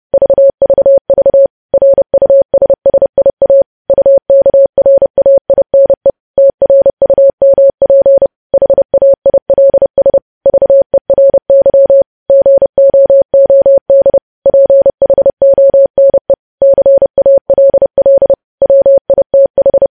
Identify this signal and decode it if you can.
Morse code, heard as a keyed tone, about 30 wpm, 550 Hz. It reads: VVV RUSSIA UKRAINE TRUMP HAILS VERY GOOD PHONE CALL WITH
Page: news headlines in morse code at 30 wpm